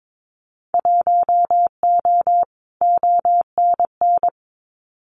Morse code, heard as a keyed tone, about 22 wpm, 700 Hz. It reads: 1O ONN
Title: Winter Field Day Exchanges Encore 22wpm